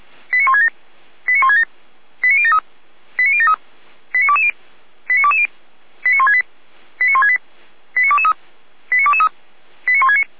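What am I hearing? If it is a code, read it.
Five-tone selective call (ZVEI), heard as quick runs of five tones, tones of about 70 ms. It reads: 89167 89167 89073 89073 89220 89220 89168 89168 89223 89223 89169
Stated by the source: ZVEI five-tone call 89123, received by SDR, from Feuerwehr Orlamünde.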